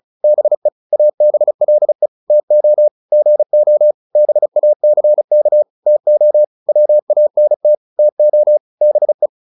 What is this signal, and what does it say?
Morse code, heard as a keyed tone, about 34 wpm, 600 Hz. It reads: BE ABLE TO GO BACK TO WANT TO BE
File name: WD205 35